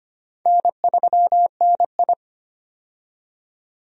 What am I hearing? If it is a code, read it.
Morse code, heard as a keyed tone, about 25 wpm, 700 Hz. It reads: N3NI
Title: USA Call Signs - 4 Characters - 25wpm